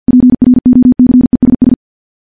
Morse code, heard as a keyed tone, about 50 wpm, 250 Hz. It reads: PROCESS
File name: morse / morse-training-50WPM